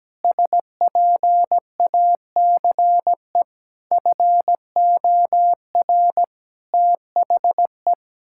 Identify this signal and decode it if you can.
Morse code, heard as a keyed tone, about 17 wpm, 700 Hz. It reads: SPACE FOR THE
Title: Sentences from Top 1000 Words - Review Part 5 - 17wpm